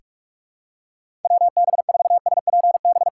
Morse code, heard as a keyed tone, about 41 wpm, 700 Hz. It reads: WB4SPB
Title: K1USN SST QSO Practice - 45wpm